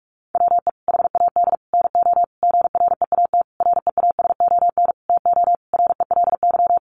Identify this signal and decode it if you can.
Morse code, heard as a keyed tone, about 44 wpm, 700 Hz. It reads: WE HAD NO GREAT REASON TO RELY
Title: Sentences from Top 1200 Words 45wpm